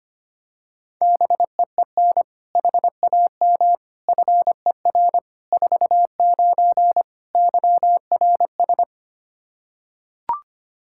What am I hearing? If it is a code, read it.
Morse code, heard as a keyed tone, about 25 wpm, 700 Hz. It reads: BEEN HAM FER 49 YRS
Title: Practice QSO - Exchange 3 - More Info - 25wpm